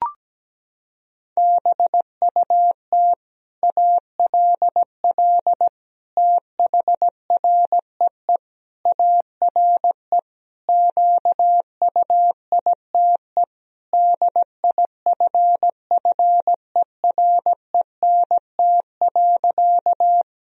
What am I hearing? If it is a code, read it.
Morse code, heard as a keyed tone, about 17 wpm, 700 Hz. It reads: BUT ALL THREE ARE QUITE DIFFERENT.
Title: Sentences from Top 1000 Words - Review Part 3 - 17wpm